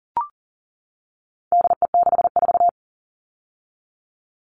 Morse code, heard as a keyed tone, about 40 wpm, 700 Hz. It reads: DE64
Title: All Cut Numbers - 4-Digit 40wpm